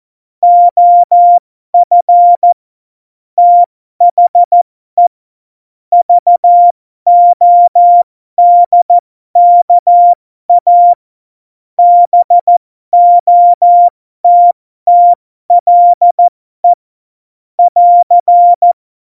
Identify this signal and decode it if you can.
Morse code, heard as a keyed tone, about 14 wpm, spaced as for 12 wpm, 700 Hz. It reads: OF THE VODKA BOTTLE <AR>
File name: Morsekurs / Witze